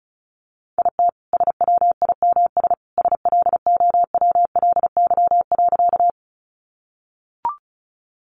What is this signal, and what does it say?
Morse code, heard as a keyed tone, about 34 wpm, 700 Hz. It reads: IT SWIMS SLOWLY.
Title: Additional Easy Phrases 35wpm